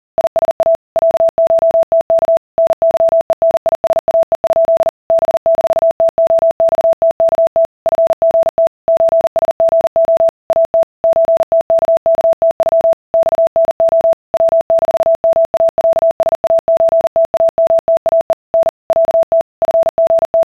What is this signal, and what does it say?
Morse code (Russian alphabet), heard as a keyed tone, about 40 wpm, 660 Hz. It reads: ИСА ЯШТК НЫЕНИСАЕ? Б-ТОЬТКТ ФГТ ЧСГО АТ ЧТККТЮ ЬНО В-МАЯСАЧТАМТАЕ Н ЯТ ФГТ